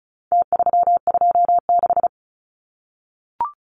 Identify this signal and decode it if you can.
Morse code, heard as a keyed tone, about 34 wpm, 700 Hz. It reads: T326
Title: Numbers - Contest Style - 4-Digit 35wpm